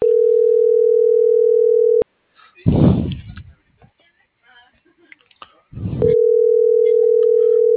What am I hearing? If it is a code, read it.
Ringback tone (North America), 440 + 480 Hz.